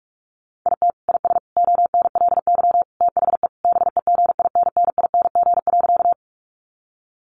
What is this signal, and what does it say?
Morse code, heard as a keyed tone, about 44 wpm, 700 Hz. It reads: IT IS ONLY THE BEGINNING.
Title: Sentences from Top 800 Words 45wpm